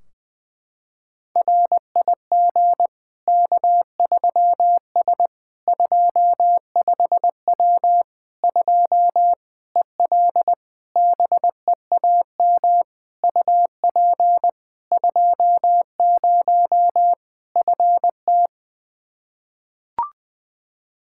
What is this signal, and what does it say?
Morse code, heard as a keyed tone, about 20 wpm, 700 Hz. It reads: RIG K3S 25W 2 EL BEAM UP 20 FT
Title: Practice QSO - Exchange 2 - Info - 20wpm